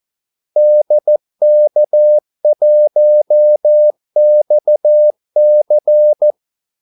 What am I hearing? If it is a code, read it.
Morse code, heard as a keyed tone, about 14 wpm, 600 Hz. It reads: DK1XC
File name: Callsigns-14wpm